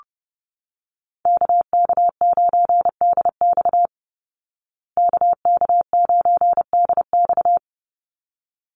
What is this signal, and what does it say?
Morse code, heard as a keyed tone, about 30 wpm, 700 Hz. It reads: KK9DX KK9DX
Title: Call Signs Repeated 3x - 30wpm